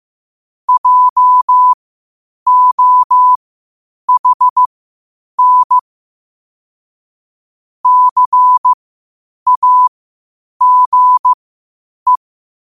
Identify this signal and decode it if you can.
Morse code, heard as a keyed tone, about 15 wpm, spaced as for 8 wpm, 1000 Hz. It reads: JOHN CAGE
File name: qotdfiles / 8wpm